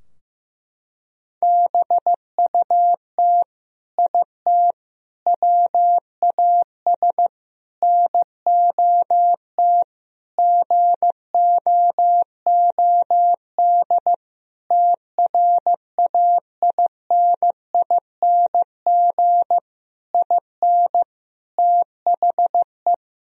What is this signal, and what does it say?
Morse code, heard as a keyed tone, about 14 wpm, 700 Hz. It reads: BUT IT WAS NOT GOOD TRAINING IN THE
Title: Sentences from Top 1000 Words - Review Part 4 - 15wpm